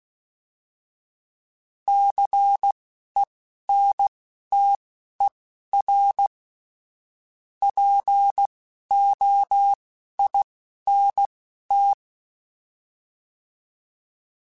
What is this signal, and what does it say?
Morse code, heard as a keyed tone, about 16 wpm, spaced as for 11 wpm, 800 Hz. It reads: CENTER POINT
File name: fromSMs / FromSMs mp3s / Cities@10